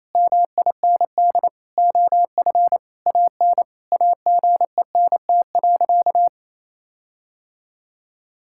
Morse code, heard as a keyed tone, about 28 wpm, 700 Hz. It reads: MIND OF AN AGENT.
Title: Sentences from Top 1600 Words 28wpm